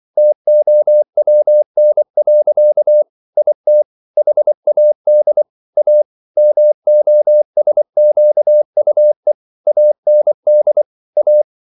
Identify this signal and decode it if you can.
Morse code, heard as a keyed tone, about 24 wpm, 600 Hz. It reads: TOWN. IT HAD A MOSQUE AND A